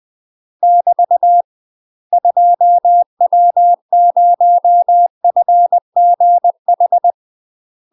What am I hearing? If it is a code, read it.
Morse code, heard as a keyed tone, about 20 wpm, 700 Hz. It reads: <BT> 2W0FGH
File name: UK MASTER QSOs TRAINING / 20WPM / UK MASTER QSOs 0004